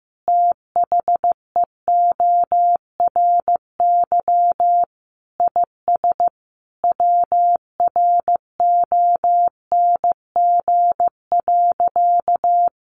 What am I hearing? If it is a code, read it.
Morse code, heard as a keyed tone, about 14 wpm, 700 Hz. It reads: THEORY IS WRONG.
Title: A Sound Of Thunder 15wpm